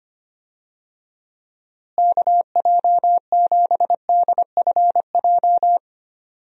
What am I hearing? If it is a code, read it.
Morse code, heard as a keyed tone, about 25 wpm, 700 Hz. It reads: KJ7DFJ
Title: USA Call Signs - 6 Characters - Repeated 2x - 25wpm